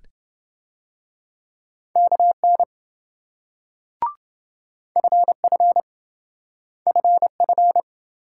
Morse code, heard as a keyed tone, about 26 wpm, 700 Hz. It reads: KN FF FF